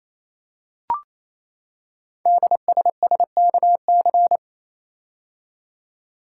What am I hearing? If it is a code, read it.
Morse code, heard as a keyed tone, about 29 wpm, 700 Hz. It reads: DSSKC
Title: Code Groups - 5 Letters - 28wpm